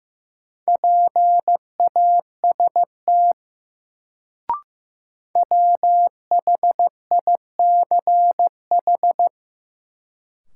Morse code, heard as a keyed tone, about 15 wpm, 700 Hz. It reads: PAST WHICH
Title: Sets of 2 Words - New Word Drill - 15wpm